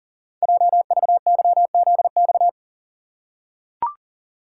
Morse code, heard as a keyed tone, about 42 wpm, 690 Hz. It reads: JVYZX
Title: Code Groups - 5 Letters - 40wpm